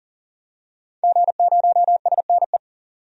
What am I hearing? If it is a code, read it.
Morse code, heard as a keyed tone, about 40 wpm, 700 Hz. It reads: G0SNE
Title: DXCC-100 Call Signs - 5 Characters - 40wpm